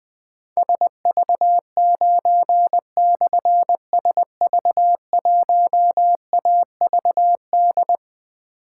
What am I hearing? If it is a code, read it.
Morse code, heard as a keyed tone, about 21 wpm, 700 Hz. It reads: SV9/SV1AVD
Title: Difficult Call Signs 20wpm